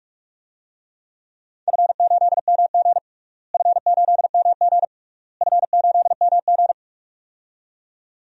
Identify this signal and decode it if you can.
Morse code, heard as a keyed tone, about 43 wpm, 700 Hz. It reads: F8MG F8MG F8MG
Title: DXCC-100 Call Signs - 4 Characters - Repeated 3x - 45wpm